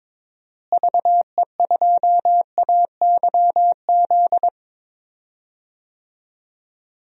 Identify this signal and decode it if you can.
Morse code, heard as a keyed tone, about 22 wpm, 700 Hz. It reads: VE2AYZ
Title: DXCC-100 Call Signs - 6 Characters - Repeated 3x - 22wpm